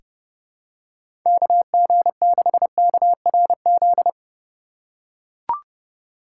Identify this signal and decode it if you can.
Morse code, heard as a keyed tone, about 32 wpm, 700 Hz. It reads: KG6KRZ
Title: USA Call Signs - 6 Characters - 4x Character Spacing - 30wpm